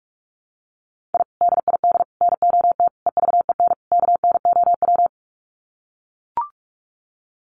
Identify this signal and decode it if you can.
Morse code, heard as a keyed tone, about 44 wpm, 700 Hz. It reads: I DID NOT EVEN KNOW
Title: Sets of 5 Words Encore 45wpm